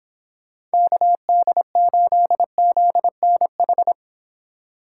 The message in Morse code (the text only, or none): KD8ZNH